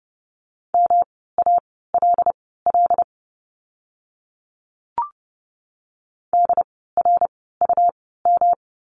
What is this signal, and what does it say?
Morse code, heard as a keyed tone, about 30 wpm, spaced as for 17 wpm, 700 Hz. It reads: MALL DRUM
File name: course / 30:10wpm / 108 Lesson 18f - Letter m - Words 30wpm